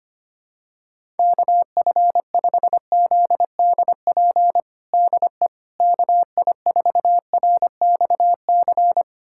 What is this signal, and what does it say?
Morse code, heard as a keyed tone, about 25 wpm, 700 Hz. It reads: KF5ZDP DE KI4RXC
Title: USA Callsign DE Callsign 25wpm